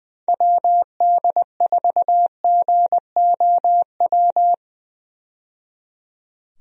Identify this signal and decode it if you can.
Morse code, heard as a keyed tone, about 20 wpm, 700 Hz. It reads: WD4GOW
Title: USA Call Signs - 6 Characters - 20wpm